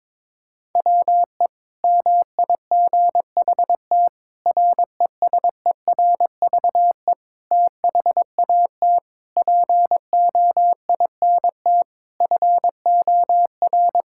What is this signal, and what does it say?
Morse code, heard as a keyed tone, about 22 wpm, 700 Hz. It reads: WE MIGHT RESERVE THAT POINT FOR
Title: Sentences from Top 1000 Words - Review Part 6 - 22wpm